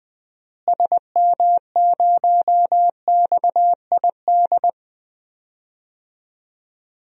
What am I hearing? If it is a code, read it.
Morse code, heard as a keyed tone, about 21 wpm, 700 Hz. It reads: SM0XID
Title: DXCC-100 Call Signs - 6 Characters - Repeated 2x - 20wpm